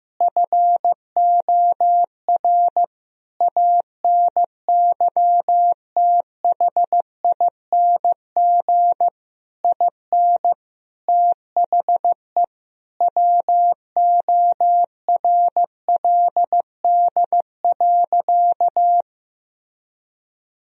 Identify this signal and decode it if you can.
Morse code, heard as a keyed tone, about 15 wpm, 700 Hz. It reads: FOR ANYTHING IN THE WORLD.
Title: Sentences from Top 1000 Words - Review Part 3 - 15wpm